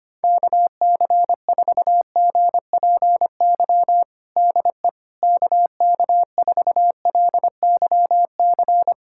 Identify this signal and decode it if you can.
Morse code, heard as a keyed tone, about 25 wpm, 700 Hz. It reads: KC4GPY DE KK4LYC
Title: USA Callsign DE Callsign - 6 Characters - 25wpm